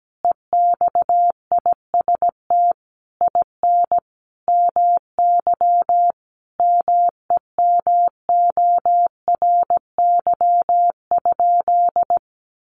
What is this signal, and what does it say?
Morse code, heard as a keyed tone, about 17 wpm, 700 Hz. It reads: EXIST IN MY MEMORY?
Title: Sentences from Top 1300 Words 17wpm